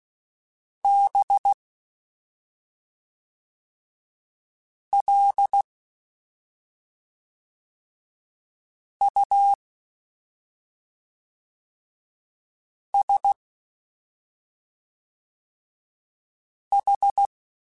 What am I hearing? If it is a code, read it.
Morse code, heard as a keyed tone, about 16 wpm, 800 Hz. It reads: B L U S H
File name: Special INGs / Special INGs-mp3s / SlowToFaster ing @5